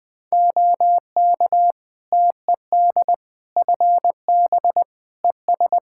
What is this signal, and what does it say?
Morse code, heard as a keyed tone, about 20 wpm, 700 Hz. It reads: OK TED FB ES
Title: Practice QSO - Exchange 2 - Info - 20wpm